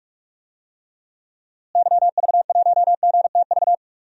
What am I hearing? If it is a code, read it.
Morse code, heard as a keyed tone, about 46 wpm, 700 Hz. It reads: YV1GTV
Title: DXCC-100 Call Signs - 6 Characters - Repeated 2x - 45wpm